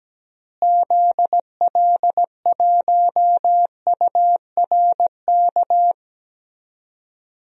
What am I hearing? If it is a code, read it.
Morse code, heard as a keyed tone, about 17 wpm, 700 Hz. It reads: ZL1URK
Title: DXCC-100 Call Signs - 6 Characters - Repeated 3x - 17wpm